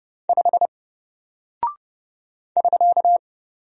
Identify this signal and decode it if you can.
Morse code, heard as a keyed tone, about 30 wpm, 700 Hz. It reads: 5 <SK>
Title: Single Character (Letter:Number:Punctuation:Prosign) Alt 30wpm